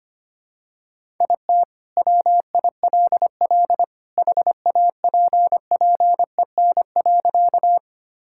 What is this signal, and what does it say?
Morse code, heard as a keyed tone, about 25 wpm, 700 Hz. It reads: IT WILL HAPPEN.
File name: practice / 25wpm / Sentences with 2-Word Phrases - 25wpm